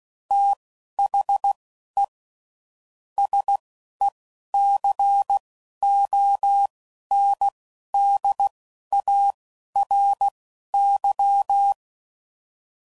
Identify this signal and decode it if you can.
Morse code, heard as a keyed tone, about 16 wpm, spaced as for 11 wpm, 800 Hz. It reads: THE SECONDARY
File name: CW Code Groups @10WPM / Technical Subjects / SwPwrSup